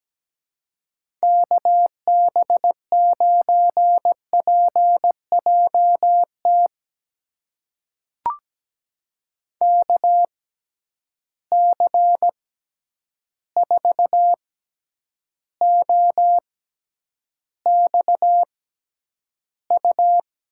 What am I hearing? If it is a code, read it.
Morse code, heard as a keyed tone, about 17 wpm, spaced as for 9 wpm, 700 Hz. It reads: KB9PJT KC4OXU
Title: USA Call Signs - 6 Characters - 6x Character Spacing - 17wpm